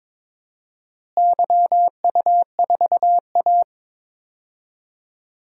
Morse code, heard as a keyed tone, about 22 wpm, 700 Hz. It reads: YU4A
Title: DXCC-100 Call Signs - 4 Characters - Repeated 3x - 22wpm